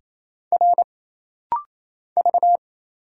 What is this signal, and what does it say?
Morse code, heard as a keyed tone, about 27 wpm, 700 Hz. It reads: R V
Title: Single Letters - Rapid-Fire 28wpm